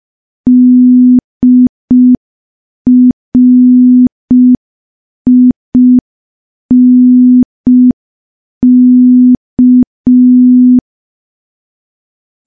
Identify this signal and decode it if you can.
Morse code, heard as a keyed tone, about 5 wpm, 250 Hz. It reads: DRINK
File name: morse / morse-training-05WPM